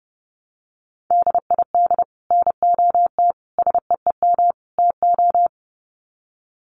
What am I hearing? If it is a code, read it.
Morse code, heard as a keyed tone, about 30 wpm, 700 Hz. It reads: DID NOT SEEM TO
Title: Sets of 4 Words 30wpm